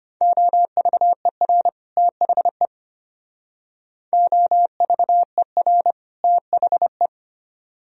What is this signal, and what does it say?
Morse code, heard as a keyed tone, about 33 wpm, 700 Hz. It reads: OVER THE OVER THE